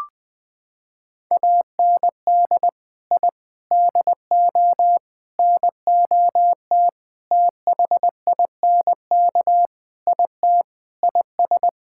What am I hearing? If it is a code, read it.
Morse code, heard as a keyed tone, about 20 wpm, 700 Hz. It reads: AND I DO NOT THINK IT IS
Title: Sentences with 5-Word Phrases - 20wpm